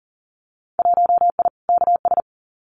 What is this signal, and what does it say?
Morse code, heard as a keyed tone, about 40 wpm, 700 Hz. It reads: 1I KS
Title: Winter Field Day Exchanges Encore 40wpm